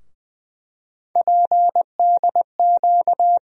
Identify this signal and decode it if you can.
Morse code, heard as a keyed tone, about 20 wpm, 700 Hz. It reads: PDQ
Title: USA Call Sign Postfixes - 3 Letters 20wpm